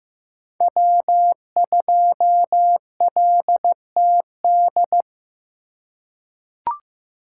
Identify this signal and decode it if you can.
Morse code, heard as a keyed tone, about 15 wpm, 700 Hz. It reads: W2LTD
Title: USA Call Signs - 5 Characters - 4x Character Spacing - 15wpm